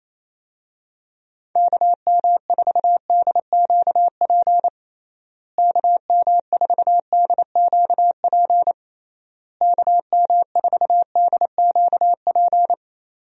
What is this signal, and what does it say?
Morse code, heard as a keyed tone, about 28 wpm, 700 Hz. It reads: KM4DQP KM4DQP KM4DQP
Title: USA Call Signs - 6 Characters - Repeated 3x - 28wpm